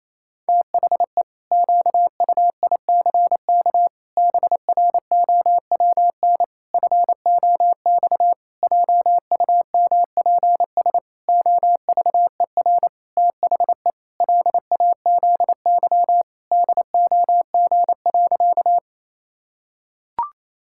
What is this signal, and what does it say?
Morse code, heard as a keyed tone, about 28 wpm, 700 Hz. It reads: THE QUICK BROWN FOX JUMPS OVER THE LAZY DOG.
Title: Pangram Challenge v2 - 28wpm